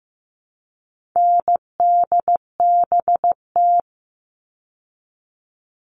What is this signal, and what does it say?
Morse code, heard as a keyed tone, about 15 wpm, 700 Hz. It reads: NDBT
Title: All Cut Numbers - 4-Digit 15wpm